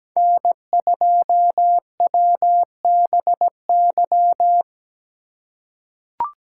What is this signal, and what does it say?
Morse code, heard as a keyed tone, about 17 wpm, 700 Hz. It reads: N2WBY
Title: USA Call Signs - 4x Character Spacing - 17wpm